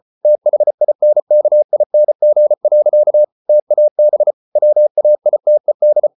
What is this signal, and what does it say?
Morse code, heard as a keyed tone, about 34 wpm, 600 Hz. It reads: THINKING. TAB WAITED